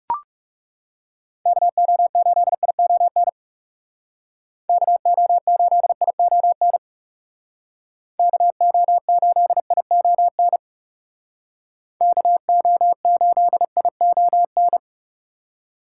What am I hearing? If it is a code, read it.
Morse code, heard as a keyed tone, about 38 wpm, 700 Hz. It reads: KO8ION KO8ION KO8ION KO8ION